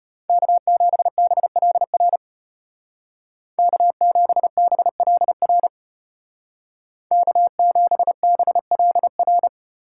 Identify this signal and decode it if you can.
Morse code, heard as a keyed tone, about 33 wpm, 700 Hz. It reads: K7BLR K7BLR K7BLR